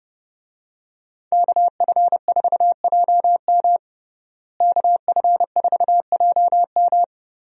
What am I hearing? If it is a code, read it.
Morse code, heard as a keyed tone, about 30 wpm, 700 Hz. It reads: KF4JM KF4JM
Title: USA Call Signs - 5 Characters - Repeated 3x - 30wpm